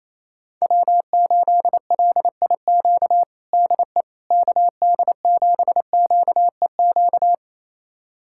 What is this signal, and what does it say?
Morse code, heard as a keyed tone, about 28 wpm, 700 Hz. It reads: W8LIQ DE KD7QEQ
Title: USA Callsign DE Callsign 28wpm